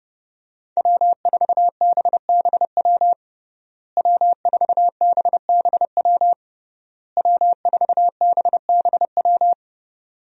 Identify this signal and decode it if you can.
Morse code, heard as a keyed tone, about 30 wpm, 700 Hz. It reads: W4BBW W4BBW W4BBW
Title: USA Call Signs - 5 Characters - Repeated 3x - 30wpm